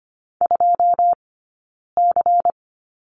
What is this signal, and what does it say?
Morse code, heard as a keyed tone, about 25 wpm, 700 Hz. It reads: 2 C
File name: Lesson 17d - Letter u - Two Characters 25wpm